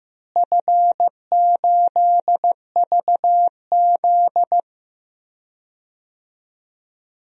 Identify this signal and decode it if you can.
Morse code, heard as a keyed tone, about 16 wpm, 700 Hz. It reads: F8VZ
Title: DXCC-100 Call Signs - 4 Characters - Repeated 2x - 15wpm